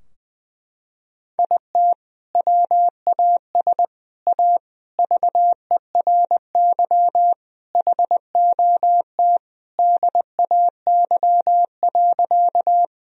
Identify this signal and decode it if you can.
Morse code, heard as a keyed tone, about 19 wpm, 700 Hz. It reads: IT WAS A VERY HOT DAY.
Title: Sentences from Top 1000 Words - Review Part 3 - 20wpm